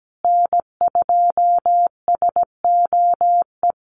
Morse code, heard as a keyed tone, about 17 wpm, 700 Hz. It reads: N2SOE
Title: USA Call Signs Repeated 2x 17wpm